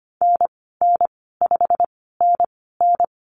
Morse code, heard as a keyed tone, about 25 wpm, spaced as for 16 wpm, 700 Hz. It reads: NN5NN
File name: Lesson 15g - Number 5 - Call Signs 25wpm